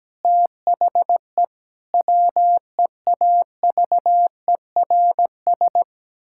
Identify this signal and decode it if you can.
Morse code, heard as a keyed tone, about 17 wpm, 700 Hz. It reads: THE WEAVERS
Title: Pangram Challenge v2 - 17wpm